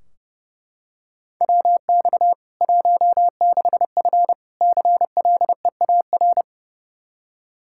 Morse code, heard as a keyed tone, about 30 wpm, 700 Hz. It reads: WX 16F CLEAR
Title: Practice QSO - Exchange 2 - Info - 30wpm